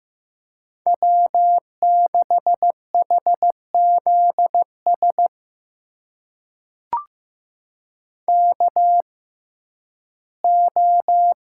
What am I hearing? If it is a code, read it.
Morse code, heard as a keyed tone, about 15 wpm, spaced as for 11 wpm, 700 Hz. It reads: W6HZS KO
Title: USA Call Signs - 5 Characters - 6x Character Spacing - 15wpm